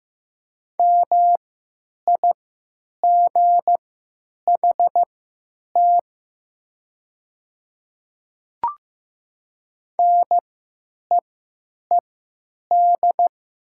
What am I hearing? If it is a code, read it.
Morse code, heard as a keyed tone, about 15 wpm, spaced as for 8 wpm, 700 Hz. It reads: MIGHT NEED
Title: 3x Character Spacing - Top 500 Words - 15wpm